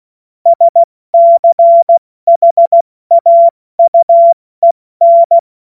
Morse code, heard as a keyed tone, about 16 wpm, 680 Hz. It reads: SCHAUEN
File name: Morsekurs / Witze / Witz 19 Dezember 2023 70 F80